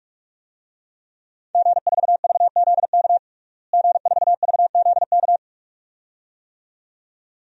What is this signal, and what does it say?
Morse code, heard as a keyed tone, about 41 wpm, 700 Hz. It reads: G4VZK G4VZK
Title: DXCC-100 Call Signs - 5 Characters - Repeated 2x - 45wpm